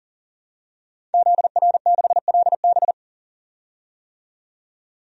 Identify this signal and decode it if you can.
Morse code, heard as a keyed tone, about 37 wpm, 700 Hz. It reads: ZR6LB
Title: DXCC-100 Call Signs - 5 Characters - 40wpm